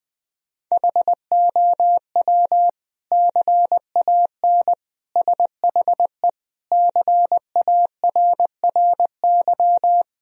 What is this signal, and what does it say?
Morse code, heard as a keyed tone, about 20 wpm, 700 Hz. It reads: HOW CAN SHE CARRY
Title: Sentences from Top 1000 Words - Review Part 3 - 20wpm